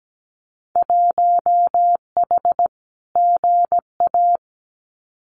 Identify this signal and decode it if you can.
Morse code, heard as a keyed tone, about 17 wpm, 700 Hz. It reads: 1H GA
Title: Winter Field Day Exchanges 17wpm